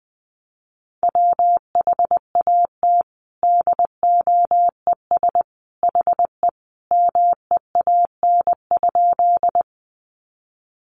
Morse code, heard as a keyed tone, about 20 wpm, 700 Hz. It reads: WHAT DOES HE MEAN?
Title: Sentences from Top 200 Words 20wpm